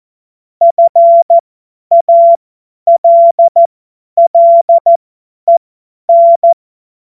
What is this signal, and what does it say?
Morse code, heard as a keyed tone, about 14 wpm, spaced as for 10 wpm, 680 Hz. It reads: FALLEN
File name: Morsekurs / Witze / Witz 19 März 2024 50 F70